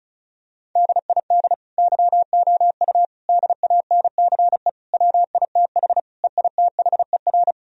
Morse code, heard as a keyed tone, about 36 wpm, 700 Hz. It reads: DID YOU DANCE WITH EITHER
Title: Sentences from Top 1000 Words - Review Part 4 - 35wpm